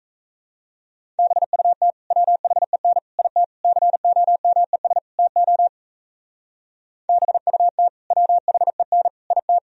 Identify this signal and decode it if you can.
Morse code, heard as a keyed tone, about 34 wpm, 700 Hz. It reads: BUT WHEN IT COMES TO BUT WHEN IT
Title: Speed-Racing - Sets of 5 Words 28wpm